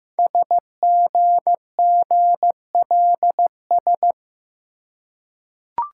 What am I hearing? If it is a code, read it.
Morse code, heard as a keyed tone, about 15 wpm, 700 Hz. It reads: SGGLS
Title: Code Groups - 5 Letters - 15wpm